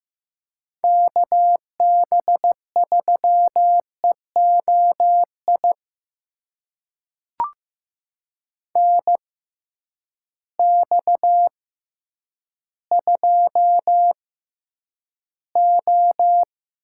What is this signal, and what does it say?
Morse code, heard as a keyed tone, about 15 wpm, spaced as for 9 wpm, 700 Hz. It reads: KB3EOI NX2O
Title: USA Call Signs - 6x Character Spacing - 15wpm